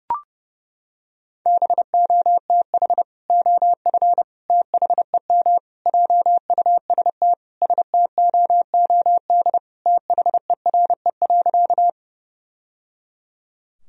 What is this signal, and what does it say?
Morse code, heard as a keyed tone, about 32 wpm, 700 Hz. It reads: BOTH OF THEM JUST STOOD THERE.
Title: Sentences from Top 1000 Words - Review Part 2 - 30wpm